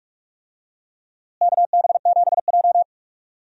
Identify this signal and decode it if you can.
Morse code, heard as a keyed tone, about 46 wpm, 700 Hz. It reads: KB7J
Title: USA Call Signs - 4 Characters - Repeated 3x - 45wpm